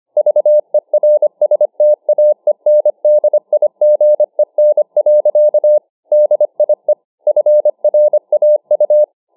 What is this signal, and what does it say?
Morse code, heard as a keyed tone, about 25 wpm, 600 Hz. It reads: VERSTAENDIGEN. DIE FRAU